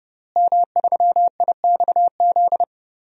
Morse code, heard as a keyed tone, about 32 wpm, 700 Hz. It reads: M3IXZ
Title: DXCC-100 Call Signs - 5 Characters - Repeated 3x - 30wpm